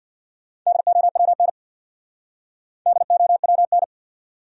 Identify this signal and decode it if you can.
Morse code, heard as a keyed tone, about 62 wpm, 700 Hz. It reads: DOWN DOWN
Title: Speed-Racing Top 100 Words 50wpm